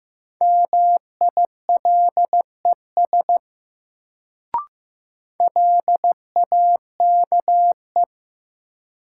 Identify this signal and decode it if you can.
Morse code, heard as a keyed tone, about 15 wpm, 700 Hz. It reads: MILES LAKE
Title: Sentences from Top 1000 Words - Review Part 3 - New Word Drill - 15wpm